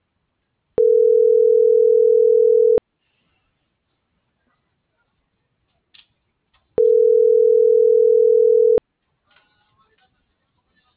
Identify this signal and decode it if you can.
Ringback tone (North America), 440 + 480 Hz.